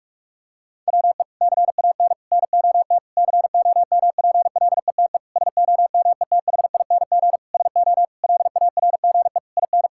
Morse code, heard as a keyed tone, about 46 wpm, 700 Hz. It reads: WE CAN NOT COMPLETE SOMETHING SO LARGE IN
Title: Sentences from Top 1000 Words - Review Part 6 - 45wpm